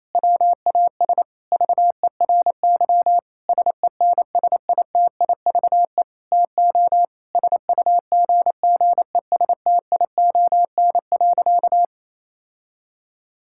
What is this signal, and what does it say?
Morse code, heard as a keyed tone, about 28 wpm, 700 Hz. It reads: WAS VERY SENSITIVE TO SUGGESTION.
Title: Sentences from Top 1600 Words 28wpm